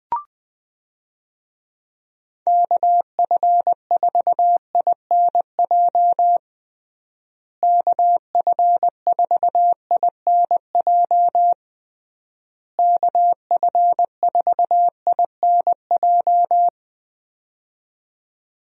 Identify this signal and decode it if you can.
Morse code, heard as a keyed tone, about 20 wpm, 700 Hz. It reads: KF4INJ KF4INJ KF4INJ
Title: USA Call Signs - 6 Characters - Repeated 3x - 20wpm